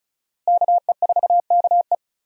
Morse code, heard as a keyed tone, about 36 wpm, 700 Hz. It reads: KE4KE
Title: USA Call Signs - 5 Characters - Repeated 2x - 35wpm